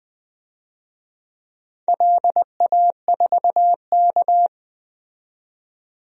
Morse code, heard as a keyed tone, about 20 wpm, 700 Hz. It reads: LA4K
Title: DXCC-100 Call Signs - 4 Characters - 20wpm